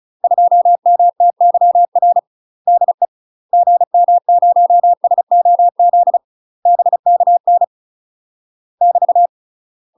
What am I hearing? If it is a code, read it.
Morse code, heard as a keyed tone, about 36 wpm, 700 Hz. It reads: 2MTYR DE GM0SOZ BKN <BT>
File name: UK MASTER QSOs TRAINING / 35WPM / UK MASTER QSOs 0024